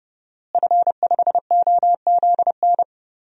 Morse code, heard as a keyed tone, about 30 wpm, 700 Hz. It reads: F5OZN
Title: DXCC-100 Call Signs - 5 Characters - Repeated 3x - 30wpm